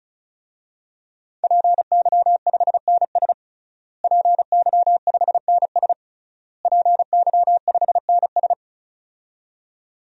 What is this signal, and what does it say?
Morse code, heard as a keyed tone, about 36 wpm, 700 Hz. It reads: PY5NS PY5NS PY5NS
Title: DXCC-100 Call Signs - 5 Characters - Repeated 3x - 35wpm